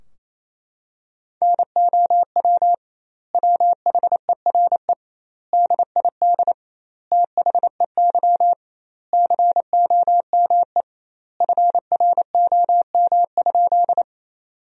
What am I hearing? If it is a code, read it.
Morse code, heard as a keyed tone, about 28 wpm, 700 Hz. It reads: NOW WHERE DID THEY COME FROM?
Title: 2x Word-Spacing - Sentences from Top 200 Words - 28wpm